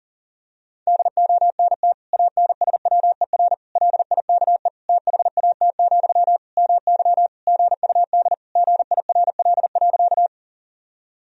Morse code, heard as a keyed tone, about 40 wpm, 700 Hz. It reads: DONT ANSWER LIKE THAT, MY GUD GIRL.
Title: Sentences from Top 400 Words and Abbreviations 40wpm